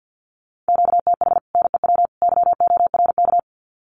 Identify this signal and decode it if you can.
Morse code, heard as a keyed tone, about 50 wpm, 700 Hz. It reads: QTH NEW YORK
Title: Common QSO Phrases 50wpm